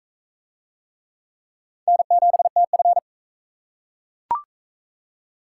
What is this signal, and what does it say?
Morse code, heard as a keyed tone, about 37 wpm, 700 Hz. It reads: N7TF